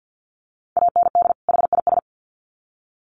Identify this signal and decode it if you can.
Morse code, heard as a keyed tone, about 50 wpm, 700 Hz. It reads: AND HIS
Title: Sets of 2 Words 50wpm